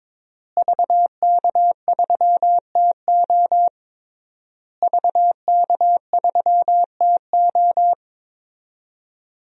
Morse code, heard as a keyed tone, about 22 wpm, 700 Hz. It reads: VK3TO VK3TO
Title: DXCC-100 Call Signs - 5 Characters - Repeated 3x - 22wpm